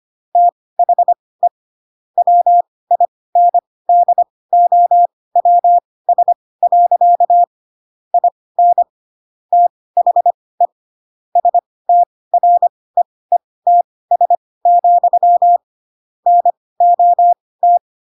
Morse code, heard as a keyed tone, about 25 wpm, spaced as for 18 wpm, 700 Hz. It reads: THE WINDOWS. IN THE STREETS, NOT
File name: Pinocchio / Pinocchio ch6 12 25 ik1hge